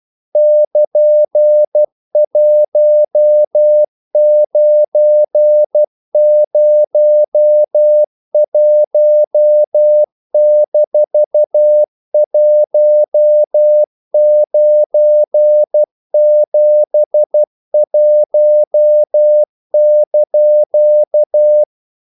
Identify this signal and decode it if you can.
Morse code, heard as a keyed tone, about 12 wpm, 600 Hz. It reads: <KN>1901-1971)
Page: English quotations in CW (telegraphy - morse)